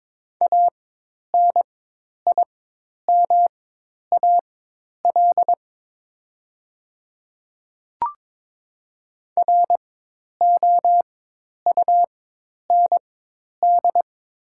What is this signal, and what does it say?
Morse code, heard as a keyed tone, about 22 wpm, spaced as for 10 wpm, 700 Hz. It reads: ANIMAL ROUND
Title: 4x Character Spacing - Top 200 Words - 22wpm